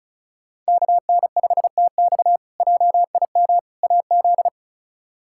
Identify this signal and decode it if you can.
Morse code, heard as a keyed tone, about 34 wpm, 700 Hz. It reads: KN5TX JIM AZ
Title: K1USN SST QSO Practice - 35wpm